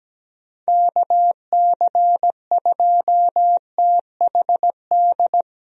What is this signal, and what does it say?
Morse code, heard as a keyed tone, about 17 wpm, 700 Hz. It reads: KC2THD
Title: USA Call Signs - 6 Characters - Repeated 3x - 17wpm